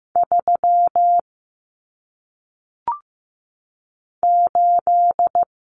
Morse code, heard as a keyed tone, about 15 wpm, 700 Hz. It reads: 3 8
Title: Numbers - 1-Digit 15wpm